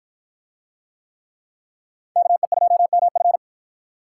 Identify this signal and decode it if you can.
Morse code, heard as a keyed tone, about 49 wpm, 700 Hz. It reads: KE2MF